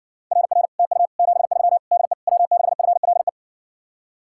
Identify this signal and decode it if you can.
Morse code, heard as a keyed tone, about 60 wpm, 700 Hz. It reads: RR TU 73 DE W6FLE